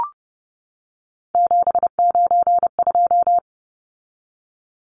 Morse code, heard as a keyed tone, about 30 wpm, 700 Hz. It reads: 792
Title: Numbers - 3-Digit 30wpm